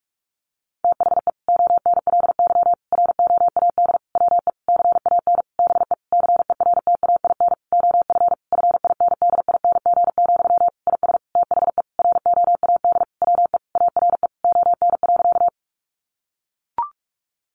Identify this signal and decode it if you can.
Morse code, heard as a keyed tone, about 44 wpm, 700 Hz. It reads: THE ONLY ROAD WE CAN BE CERTAIN OF FINDING, IS THE ROAD WE ARE ON.
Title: Sentences from Top 1100 Words 45wpm